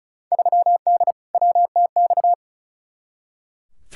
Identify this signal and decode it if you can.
Morse code, heard as a keyed tone, about 36 wpm, 700 Hz. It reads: 3D WTX
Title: Field Day Exchanges Encore - 35wpm